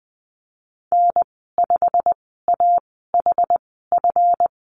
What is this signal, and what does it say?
Morse code, heard as a keyed tone, about 20 wpm, spaced as for 15 wpm, 700 Hz. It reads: N5AHF
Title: 154 Lesson 25g - Letter p - Call Signs 20wpm